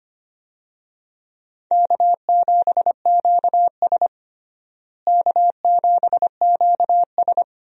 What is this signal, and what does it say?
Morse code, heard as a keyed tone, about 25 wpm, 700 Hz. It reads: K7QS K7QS
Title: USA Call Signs - 4 Characters - Repeated 3x - 25wpm